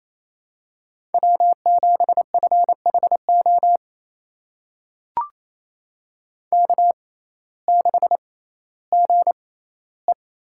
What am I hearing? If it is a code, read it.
Morse code, heard as a keyed tone, about 28 wpm, spaced as for 16 wpm, 700 Hz. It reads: W7FHO K6GE
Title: USA Call Signs - 6x Character Spacing - 28wpm